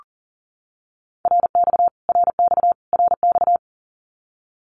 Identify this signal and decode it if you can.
Morse code, heard as a keyed tone, about 40 wpm, 700 Hz. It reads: RX RX RX
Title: Abbreviations Repeated 3x - 40wpm